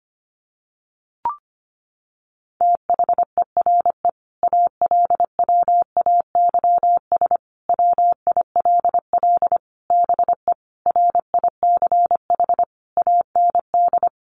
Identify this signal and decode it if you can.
Morse code, heard as a keyed tone, about 25 wpm, 700 Hz. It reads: THERE ALWAYS WILL BE RICH AND
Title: Sentences from Top 1100 Words 25wpm